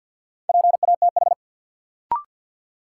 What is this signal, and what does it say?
Morse code, heard as a keyed tone, about 52 wpm, 700 Hz. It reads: PATH
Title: Sentences from Top 1400 Words - New Word Drill - 50wpm